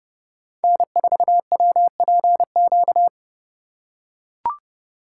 Morse code, heard as a keyed tone, about 30 wpm, 700 Hz. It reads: N4WPQ
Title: USA Call Signs - 5 Characters - 4x Character Spacing - 30wpm